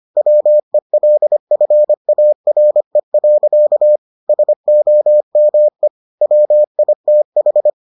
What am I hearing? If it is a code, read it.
Morse code, heard as a keyed tone, about 25 wpm, 600 Hz. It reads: WELFARE. SOME WITH